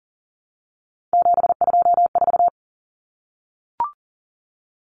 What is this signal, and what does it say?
Morse code, heard as a keyed tone, about 40 wpm, 700 Hz. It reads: 724
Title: Numbers - 3-Digit 40wpm